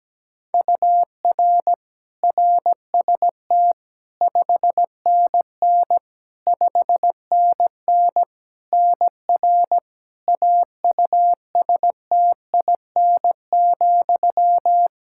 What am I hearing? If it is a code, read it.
Morse code, heard as a keyed tone, about 17 wpm, 700 Hz. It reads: UR RST 5NN 5NN NR AUSTIN,
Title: Practice QSO - Exchange 1 - Beginning - 17wpm